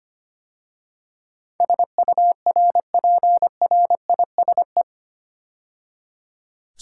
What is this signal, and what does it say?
Morse code, heard as a keyed tone, about 25 wpm, 690 Hz. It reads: SURPRISE
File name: 8 Letter Words - First 3 Letters Given - 25wpm